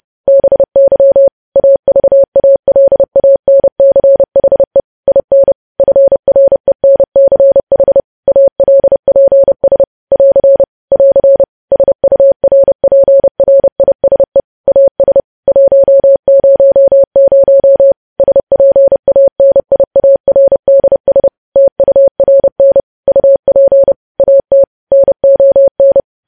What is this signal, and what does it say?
Morse code, heard as a keyed tone, about 30 wpm, 550 Hz. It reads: BY AVALANCHE IN FRENCH ALPS <AR> <AR> SURPRISE AS 100 SPANIARDS TURN UP AT NON
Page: News Headlines in Morse Code at 30 WPM